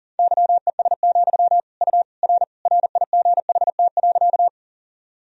Morse code, heard as a keyed tone, about 40 wpm, 700 Hz. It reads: YES, U R RIGHT.
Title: Sentences from Top 500 Words and Abbreviations 40wpm